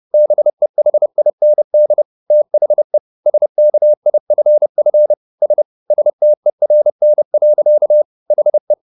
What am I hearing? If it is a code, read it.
Morse code, heard as a keyed tone, about 30 wpm, 600 Hz. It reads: BEHIND THE SKIFF S STERN. HE